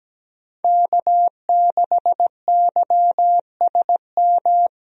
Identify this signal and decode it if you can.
Morse code, heard as a keyed tone, about 17 wpm, 700 Hz. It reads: K6YSM
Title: USA Call Signs - 5 Characters - Repeated 3x - 17wpm